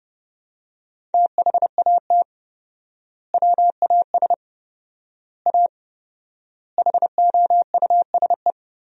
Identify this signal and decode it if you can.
Morse code, heard as a keyed tone, about 30 wpm, 700 Hz. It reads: THAT WAS A HOUSE
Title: 4x Word-Spacing - Sentences from Top 200 Words - 30wpm